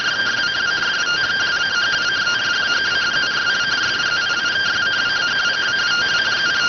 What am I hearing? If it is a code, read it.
Two-tone teleprinter signal, about 170 Hz shift.